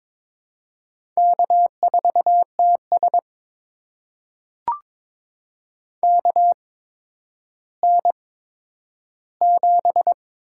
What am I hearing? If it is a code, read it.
Morse code, heard as a keyed tone, about 22 wpm, spaced as for 11 wpm, 700 Hz. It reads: K4TS KN7
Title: USA Call Signs - 8x Character Spacing - 22wpm